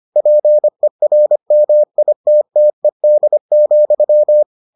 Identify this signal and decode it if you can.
Morse code, heard as a keyed tone, about 25 wpm, 600 Hz. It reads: PERMITTED,